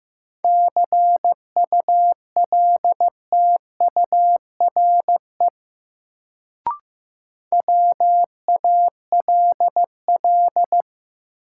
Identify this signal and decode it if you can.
Morse code, heard as a keyed tone, about 15 wpm, 700 Hz. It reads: CULTURE WALL
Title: Sentences from Top 1000 Words - Review Part 6 - New Word Drill - 15wpm